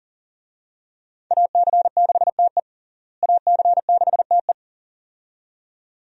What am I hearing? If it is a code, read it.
Morse code, heard as a keyed tone, about 40 wpm, 700 Hz. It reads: AC6TE AC6TE
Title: USA Call Signs - 5 Characters - Repeated 2x - 40wpm